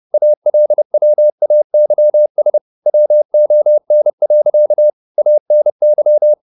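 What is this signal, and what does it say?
Morse code, heard as a keyed tone, about 30 wpm, 600 Hz. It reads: ALWAYS WON. ANY
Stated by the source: Morse code at 30 wpm, the novel Never by Ken Follett in English, from CW med Gnister.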